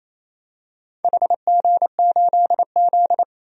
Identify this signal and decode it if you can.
Morse code, heard as a keyed tone, about 28 wpm, 700 Hz. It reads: HG8Z
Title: DXCC-100 Call Signs - 4 Characters - 28wpm